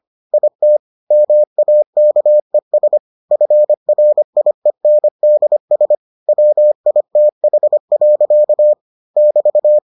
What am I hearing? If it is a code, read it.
Morse code, heard as a keyed tone, about 25 wpm, 600 Hz. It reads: IT MAKES FRIENDS WITH. <BT>